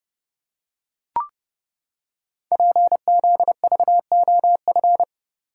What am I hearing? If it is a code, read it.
Morse code, heard as a keyed tone, about 30 wpm, 700 Hz. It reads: PZVOF
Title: Code Groups - 5 Letters - 30wpm